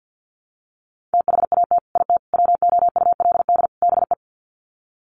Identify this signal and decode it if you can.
Morse code, heard as a keyed tone, about 50 wpm, 700 Hz. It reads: THAT IT WOULD BE
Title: Sets of 4 Words 50wpm